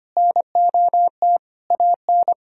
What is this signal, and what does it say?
Morse code, heard as a keyed tone, about 25 wpm, 700 Hz. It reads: NOT AN
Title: Sentences from Top 1000 Words - Review Part 2 - 25wpm